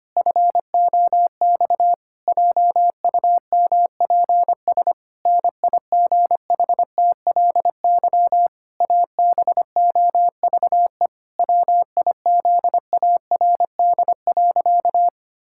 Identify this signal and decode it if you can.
Morse code, heard as a keyed tone, about 25 wpm, 700 Hz. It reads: FOX JUMPS NIGHTLY ABOVE WIZARD.
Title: Pangram Challenge v2 - 25wpm